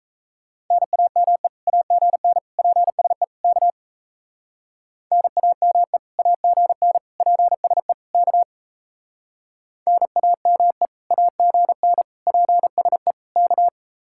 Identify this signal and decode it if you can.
Morse code, heard as a keyed tone, about 33 wpm, 700 Hz. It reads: NAME AGN PSE K NAME AGN PSE K NAME AGN PSE K
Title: Speed-Racing Common QSO Phrases Encore - 28wpm